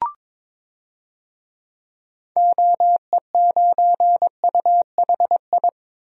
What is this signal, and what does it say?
Morse code, heard as a keyed tone, about 22 wpm, 700 Hz. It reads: OE9UHI
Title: DXCC-100 Call Signs - 6 Characters - Repeated 2x - 22wpm